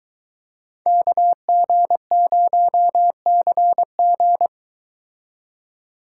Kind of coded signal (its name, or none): Morse code